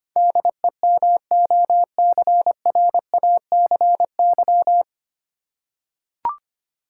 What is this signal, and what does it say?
Morse code, heard as a keyed tone, about 25 wpm, 700 Hz. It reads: DEMOCRACY
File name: Nine Letter Words - 25wpm